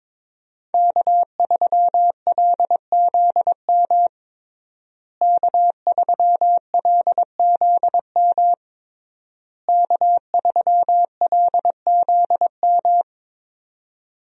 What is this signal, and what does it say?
Morse code, heard as a keyed tone, about 22 wpm, 700 Hz. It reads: K3LZM K3LZM K3LZM
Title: USA Call Signs - 5 Characters - Repeated 3x - 22wpm